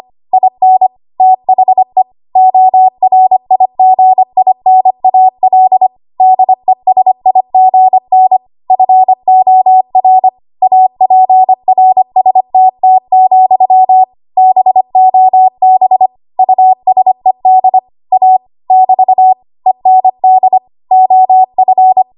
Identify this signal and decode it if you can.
Morse code, heard as a keyed tone, about 25 wpm, 750 Hz. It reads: IN THE ORIGINAL DESIGN FOR APRSTT, BOB USED A <BT> END OF
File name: Morse / 25WPM